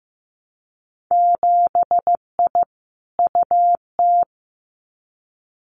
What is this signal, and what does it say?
Morse code, heard as a keyed tone, about 15 wpm, 700 Hz. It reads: 7I UT
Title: Winter Field Day Exchanges Encore 15wpm